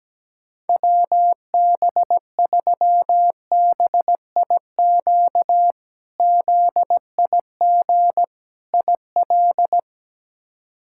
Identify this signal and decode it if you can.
Morse code, heard as a keyed tone, about 17 wpm, 700 Hz. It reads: WB3BIQ ZIG IL
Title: K1USN SST QSO Practice - 17wpm